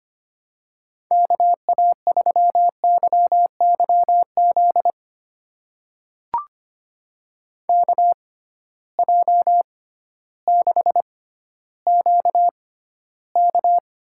Morse code, heard as a keyed tone, about 25 wpm, spaced as for 14 wpm, 700 Hz. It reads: KA3YYZ KJ6QK
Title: USA Call Signs - 6 Characters - 6x Character Spacing - 25wpm